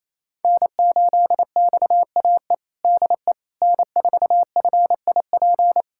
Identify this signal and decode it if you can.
Morse code, heard as a keyed tone, about 28 wpm, 700 Hz. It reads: N8XAE DE N4FIP
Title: USA Callsign DE Callsign - 5 Characters - 28wpm